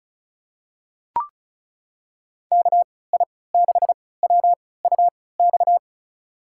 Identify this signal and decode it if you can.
Morse code, heard as a keyed tone, about 36 wpm, spaced as for 20 wpm, 700 Hz. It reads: KI6WUX
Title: USA Call Signs - 3x Character Spacing - 35wpm